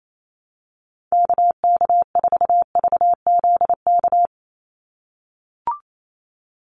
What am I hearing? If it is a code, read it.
Morse code, heard as a keyed tone, about 28 wpm, 700 Hz. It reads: KK4VZK
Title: USA Call Signs 28wpm